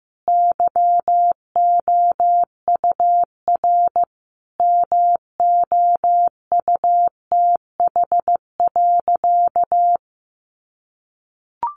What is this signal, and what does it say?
Morse code, heard as a keyed tone, about 15 wpm, 700 Hz. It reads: YOUR MOUTH.
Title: Sentences from Top 1500 Words 15wpm